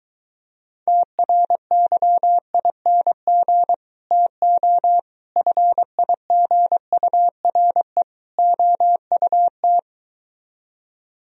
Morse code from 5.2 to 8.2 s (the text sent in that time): FIGURE